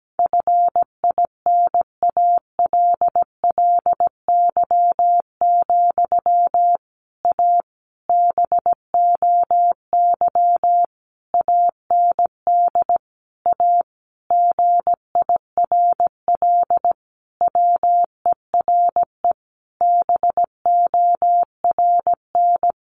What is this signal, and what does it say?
Morse code, heard as a keyed tone, about 17 wpm, 700 Hz. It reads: FINALLY, A BOY AND A GIRL WERE BORN
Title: Sentences from Top 1200 Words 17wpm